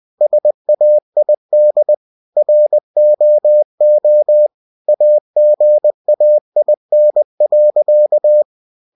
Morse code, heard as a keyed tone, about 20 wpm, 600 Hz. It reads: SAID ROO AGAIN.
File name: cw-books / winnie-the-pooh / winnie-20wpm-page-0108